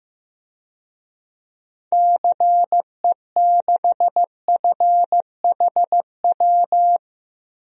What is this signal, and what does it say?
Morse code, heard as a keyed tone, about 15 wpm, 700 Hz. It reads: CE6FHW
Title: DXCC-100 Call Signs - 6 Characters - 15wpm